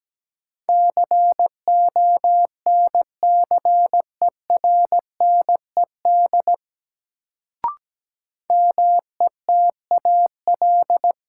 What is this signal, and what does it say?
Morse code, heard as a keyed tone, about 17 wpm, 700 Hz. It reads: CONCERNED METAL
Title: Sentences from Top 1000 Words - Review Part 6 - New Word Drill - 17wpm